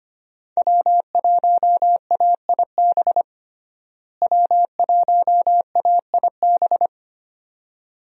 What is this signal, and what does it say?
Morse code, heard as a keyed tone, about 25 wpm, 700 Hz. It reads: W1AIB W1AIB
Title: USA Call Signs - 5 Characters - Repeated 2x - 25wpm